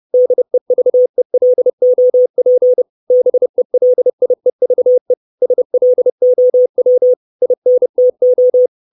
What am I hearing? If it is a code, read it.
Morse code, heard as a keyed tone, about 30 wpm, 500 Hz. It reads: DEVELOP BELIEVE SLOW INTO
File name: 632 Words 30WPM